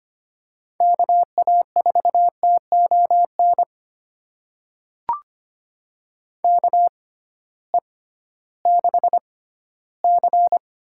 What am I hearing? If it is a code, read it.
Morse code, heard as a keyed tone, about 25 wpm, spaced as for 15 wpm, 700 Hz. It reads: KA4TON KE6C
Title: USA Call Signs - 6 Characters - 6x Character Spacing - 25wpm